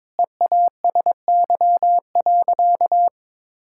Morse code, heard as a keyed tone, about 21 wpm, 700 Hz. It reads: EASY.
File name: Sentences with 7-Word Phrases - 22wpm